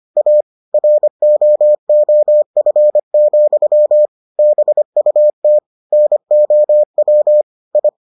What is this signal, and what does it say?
Morse code, heard as a keyed tone, about 25 wpm, 600 Hz. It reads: A ROOF, BUT NOW I